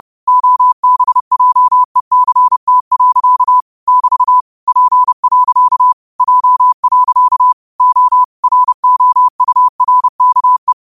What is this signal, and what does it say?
Morse code, heard as a keyed tone, about 30 wpm, 1000 Hz. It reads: OBJECT. <BT> P. J. O ROURKE
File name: qotdfiles / 30wpm